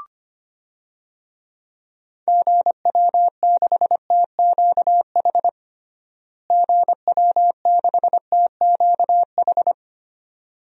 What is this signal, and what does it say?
Morse code, heard as a keyed tone, about 25 wpm, 700 Hz. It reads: GW6TQH GW6TQH
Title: DXCC-100 Call Signs - 6 Characters - Repeated 2x - 25wpm